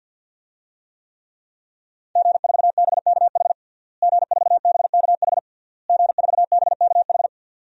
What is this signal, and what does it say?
Morse code, heard as a keyed tone, about 52 wpm, 700 Hz. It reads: G4BKH G4BKH G4BKH
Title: DXCC-100 Call Signs - 5 Characters - Repeated 3x - 50wpm